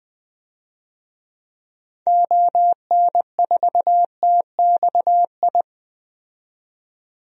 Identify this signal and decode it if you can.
Morse code, heard as a keyed tone, about 21 wpm, 700 Hz. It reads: ON4TXI
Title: DXCC-100 Call Signs - 6 Characters - 20wpm